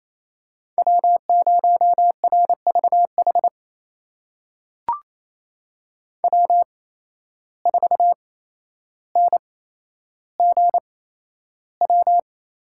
Morse code, heard as a keyed tone, about 28 wpm, spaced as for 12 wpm, 700 Hz. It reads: W0RVH W4NGW
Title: USA Call Signs - 5 Characters - 8x Character Spacing - 28wpm